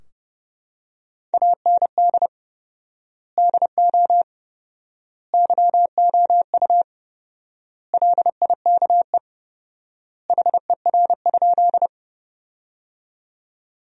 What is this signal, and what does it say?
Morse code, heard as a keyed tone, about 32 wpm, 700 Hz. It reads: AND DO YOU LIKE HER?
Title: 4x Word-Spacing - Sentences from Top 100 Words - 30wpm